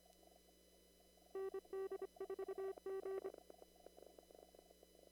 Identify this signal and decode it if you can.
Morse code, heard as a keyed tone, about 25 wpm, 1160 Hz, with noise. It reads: ND4G
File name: CW / kt5j wpx cw-op-k5pi